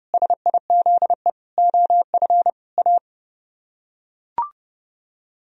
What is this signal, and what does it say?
Morse code, heard as a keyed tone, about 29 wpm, 700 Hz. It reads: SIZE OF A